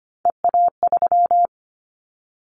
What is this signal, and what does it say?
Morse code, heard as a keyed tone, about 25 wpm, 700 Hz. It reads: EA3
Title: Callsign Prefixes for Top 25 DXCC - 25wpm